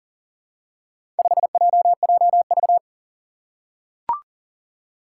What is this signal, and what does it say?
Morse code, heard as a keyed tone, about 40 wpm, 700 Hz. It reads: 5JJV
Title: Four Challenging Letter-Number Combinations 40wpm